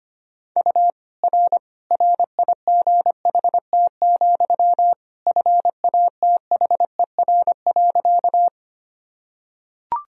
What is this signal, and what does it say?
Morse code, heard as a keyed tone, about 25 wpm, 700 Hz. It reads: U R RIGHT, FATHER.
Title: Sentences from Top 300 Words and Abbreviations 25wpm